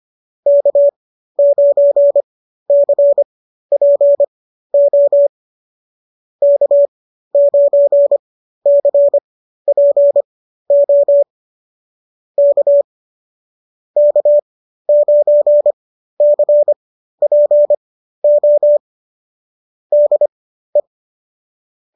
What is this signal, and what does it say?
Morse code, heard as a keyed tone, about 25 wpm, spaced as for 13 wpm, 590 Hz. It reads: K9CPO K9CPO K K9CPO DE
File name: qso211 13